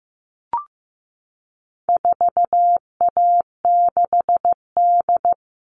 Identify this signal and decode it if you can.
Morse code, heard as a keyed tone, about 15 wpm, 700 Hz. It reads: 4A6D
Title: All Cut Numbers - 4-Digit 15wpm